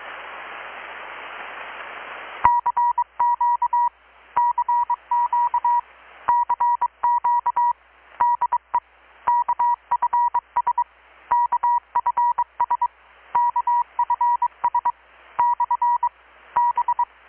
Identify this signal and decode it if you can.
Morse code, heard as a keyed tone, about 23 wpm, 960 Hz. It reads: CQ CQ CQ DE KFS KFS KFS / B
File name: blog / swl / 6-abril-2019-12695-USB-KPS